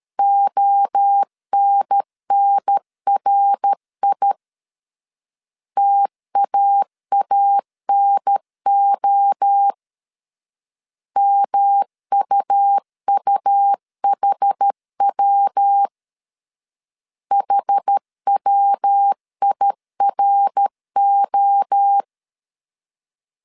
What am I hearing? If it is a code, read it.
Morse code, heard as a keyed tone, about 13 wpm, 800 Hz. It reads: ONNRI TAANO MUUHW HWIRO
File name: CW / DET / 1 3 1500C S B 12wpm 2828